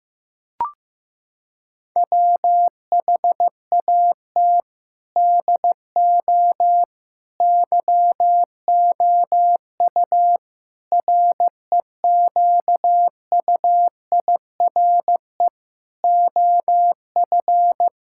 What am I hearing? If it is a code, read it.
Morse code, heard as a keyed tone, about 15 wpm, 700 Hz. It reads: WHAT DO YOU REQUIRE OF
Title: Sentences from Top 1000 Words - Review Part 6 - 15wpm